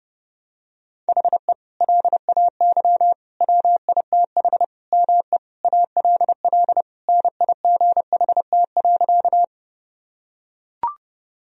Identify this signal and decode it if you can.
Morse code, heard as a keyed tone, about 30 wpm, 700 Hz. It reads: HE LAY WITH ME ALL NIGHT.
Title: Sentences from Top 1000 Words - Review Part 2 - 30wpm